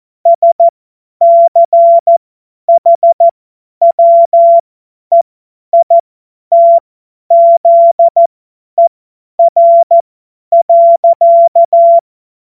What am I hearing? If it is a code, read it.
Morse code, heard as a keyed tone, about 14 wpm, spaced as for 10 wpm, 680 Hz. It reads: SCHWEITZER.
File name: Morsekurs / Witze / Witz 14. Dezember 2021, 50 F70